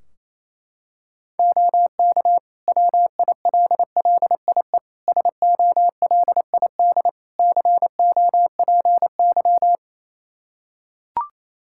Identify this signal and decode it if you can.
Morse code, heard as a keyed tone, about 28 wpm, 700 Hz. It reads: OK WILLIE SOLID COPY
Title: Practice QSO - Exchange 3 - More Info - 28wpm